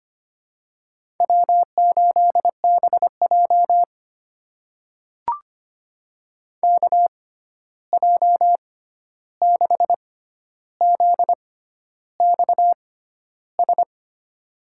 Morse code, heard as a keyed tone, about 25 wpm, spaced as for 11 wpm, 700 Hz. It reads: W8BJ KJ6ZXS
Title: USA Call Signs - 6x Character Spacing - 25wpm